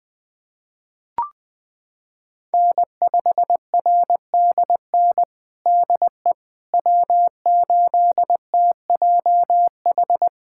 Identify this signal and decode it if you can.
Morse code, heard as a keyed tone, about 20 wpm, 700 Hz. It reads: N5RDN DE W8TJH
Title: USA Callsign DE Callsign - 5 Characters - 20wpm